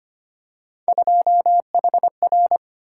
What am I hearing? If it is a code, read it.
Morse code, heard as a keyed tone, about 25 wpm, 700 Hz. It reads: 2HR
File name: Three Letter-Number Combinations - 25wpm